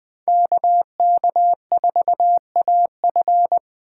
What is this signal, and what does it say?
Morse code, heard as a keyed tone, about 20 wpm, 700 Hz. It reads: KK4AF
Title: Speed Racing - USA 2x2 Call Signs 17wpm